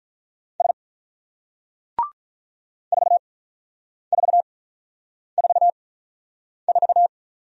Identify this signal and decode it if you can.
Morse code, heard as a keyed tone, about 48 wpm, spaced as for 9 wpm, 700 Hz. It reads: S 4444